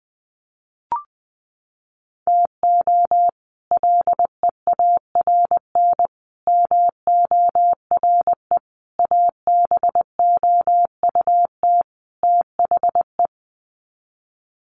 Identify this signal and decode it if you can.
Morse code, heard as a keyed tone, about 20 wpm, 700 Hz. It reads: TO LEARN MORE ABOUT THE
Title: Sets of 5 Words Encore2 20wpm